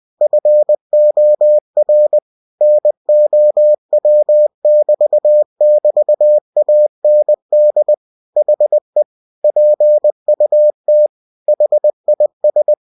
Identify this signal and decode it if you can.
Morse code, heard as a keyed tone, about 20 wpm, 600 Hz. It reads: FOR NOW<BT><BT>AND HE PUT HIS
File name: cw-books / winnie-the-pooh / winnie-20wpm-page-0107